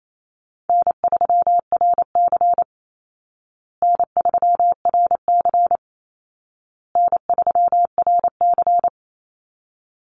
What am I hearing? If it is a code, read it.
Morse code, heard as a keyed tone, about 28 wpm, 700 Hz. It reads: N3RC N3RC N3RC
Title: Call Signs Repeated 2x - 28wpm